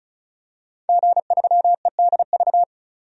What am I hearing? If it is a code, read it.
Morse code, heard as a keyed tone, about 36 wpm, 700 Hz. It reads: G3EDV
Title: DXCC-100 Call Signs - 5 Characters - 35wpm